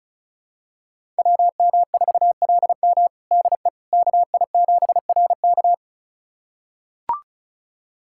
Morse code, heard as a keyed tone, about 36 wpm, 700 Hz. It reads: WM4LM DE KI7RK
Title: USA Callsign DE Callsign - 5 Characters - 35wpm